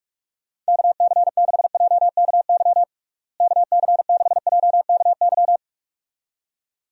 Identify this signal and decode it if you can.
Morse code, heard as a keyed tone, about 44 wpm, 700 Hz. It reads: KC6JKY KC6JKY
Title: USA Call Signs - 6 Characters - Repeated 2x - 45wpm